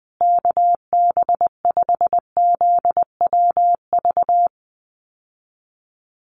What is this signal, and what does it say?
Morse code, heard as a keyed tone, about 20 wpm, 700 Hz. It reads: KB5ZWV
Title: USA Call Signs Repeated 3x 20wpm